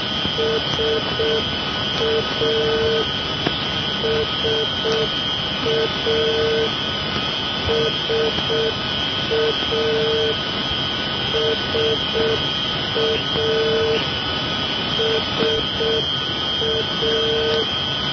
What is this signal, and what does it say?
Morse code, heard as a keyed tone, about 6 wpm, 450 Hz, with noise. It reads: SA SA SA SA SA
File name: SA NDB 1988